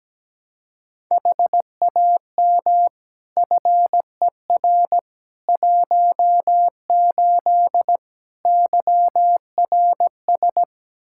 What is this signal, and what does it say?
Morse code, heard as a keyed tone, about 17 wpm, 700 Hz. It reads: HAM FER 18 YRS
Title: Practice QSO - Exchange 3 - More Info - 17wpm